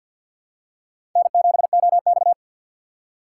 Morse code, heard as a keyed tone, about 52 wpm, 700 Hz. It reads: N7OX
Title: USA Call Signs - 4 Characters - 3x Character Spacing - 50wpm